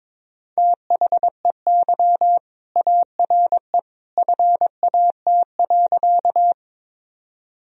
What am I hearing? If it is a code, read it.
Morse code, heard as a keyed tone, about 22 wpm, 700 Hz. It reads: THEY ARE FAT.
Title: Sentences from Top 1000 Words - Review Part 4 - 22wpm